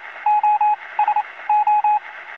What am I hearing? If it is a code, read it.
Morse code, heard as a keyed tone, about 28 wpm, spaced as for 20 wpm, 840 Hz, with radio noise. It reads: OSO
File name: radio signals / libsounds.com - Morse code SOS